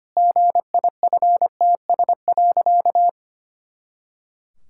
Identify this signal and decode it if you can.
Morse code, heard as a keyed tone, about 25 wpm, 700 Hz. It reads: GIFTS.
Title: Pangram Challenge v2 - 25wpm